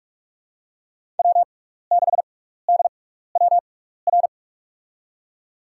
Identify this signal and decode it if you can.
Morse code, heard as a keyed tone, about 43 wpm, spaced as for 15 wpm, 700 Hz. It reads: W6DWR
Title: USA Call Signs - 5 Characters - 6x Character Spacing - 45wpm